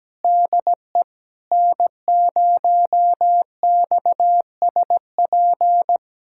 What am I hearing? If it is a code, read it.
Morse code, heard as a keyed tone, about 17 wpm, 700 Hz. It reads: DE N0XSP
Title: USA Callsign DE Callsign - 5 Characters - 17wpm